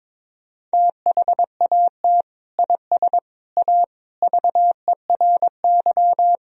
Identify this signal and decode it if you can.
Morse code, heard as a keyed tone, about 22 wpm, 700 Hz. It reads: THAT IS A VERY
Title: Sentences from Top 1000 Words - Review Part 4 - 22wpm